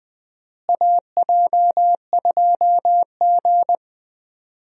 Morse code, heard as a keyed tone, about 20 wpm, 700 Hz. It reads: AJ2G
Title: USA Call Signs - 4 Characters - Repeated 2x - 20wpm